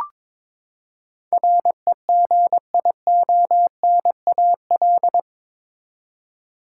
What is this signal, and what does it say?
Morse code, heard as a keyed tone, about 21 wpm, 700 Hz. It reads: REGIONAL
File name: practice / 22wpm / Eight Letter Words - 22wpm